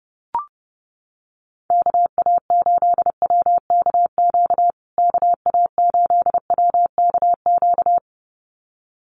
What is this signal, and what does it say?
Morse code, heard as a keyed tone, about 30 wpm, 700 Hz. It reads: KA8WKQ KA8WKQ
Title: USA Call Signs Repeated 2x 30wpm